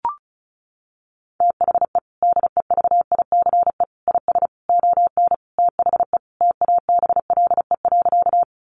Morse code, heard as a keyed tone, about 34 wpm, 700 Hz. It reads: THE DEVICE IS ON THE TABLE.
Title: Sentences from Top 1500 Words 35wpm